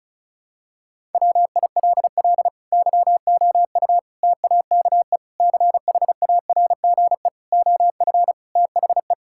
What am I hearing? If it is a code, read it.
Morse code, heard as a keyed tone, about 34 wpm, 700 Hz. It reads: WILL YOU TAKE CHARGE OF THE
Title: Sentences from Top 1000 Words - Review Part 6 - 35wpm